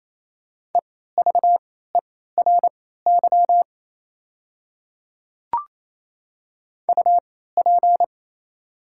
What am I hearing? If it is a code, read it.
Morse code, heard as a keyed tone, about 28 wpm, spaced as for 16 wpm, 700 Hz. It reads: EVERY UP
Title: 3x Character Spacing - Top 200 Words - 28wpm